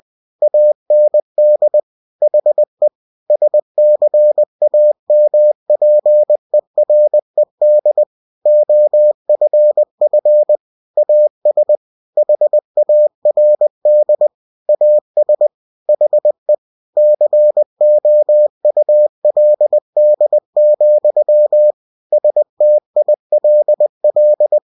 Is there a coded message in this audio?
Yes: Morse code from 0.4 to 24.6 s, heard as a keyed tone, about 20 wpm, 600 Hz. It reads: AND HE SCAMPERED OFF AS HARD AS HE COULD, STILL